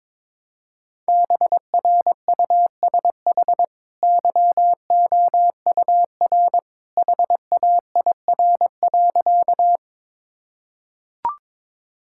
Morse code, heard as a keyed tone, about 22 wpm, 700 Hz. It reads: BRUSH YOUR HAIR.
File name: practice / 22wpm / Easy Sentences with Hints - 22wpm